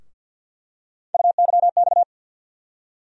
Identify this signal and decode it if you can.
Morse code, heard as a keyed tone, about 47 wpm, 700 Hz. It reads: UYX
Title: USA Call Sign Postfixes - 3 Letters 50wpm